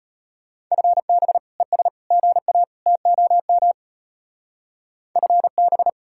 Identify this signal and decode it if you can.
Morse code, heard as a keyed tone, about 33 wpm, 700 Hz. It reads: FB ES GA TOM FB